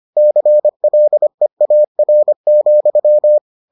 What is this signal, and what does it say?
Morse code, heard as a keyed tone, about 25 wpm, 600 Hz. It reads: CLEAR,